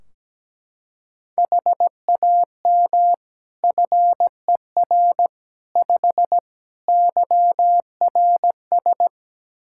Morse code, heard as a keyed tone, about 17 wpm, 700 Hz. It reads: HAM FER 5 YRS
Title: K1USN SST QSO Practice 17wpm